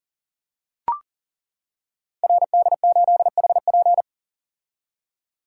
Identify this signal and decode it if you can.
Morse code, heard as a keyed tone, about 38 wpm, 700 Hz. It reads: RD8HP
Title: Code Groups - 5 Letter:Numbers - 40wpm